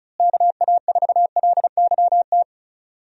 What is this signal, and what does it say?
Morse code, heard as a keyed tone, about 34 wpm, 700 Hz. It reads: KA4LYT
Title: USA Call Signs - 6 Characters - Repeated 3x - 35wpm